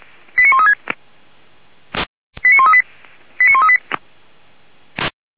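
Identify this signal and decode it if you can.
Five-tone selective call (ZVEI), heard as quick runs of five tones, tones of about 70 ms. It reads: 89136 89138 89138
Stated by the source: ZVEI five-tone selective call 89135 received on SDR by feuerwehr-orlamuende (F85360) fire brigade.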